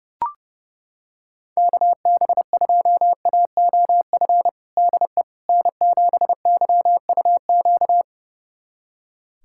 Morse code, heard as a keyed tone, about 30 wpm, 700 Hz. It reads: KB2AOF DE N7YUQ
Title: USA Callsign DE Callsign 30wpm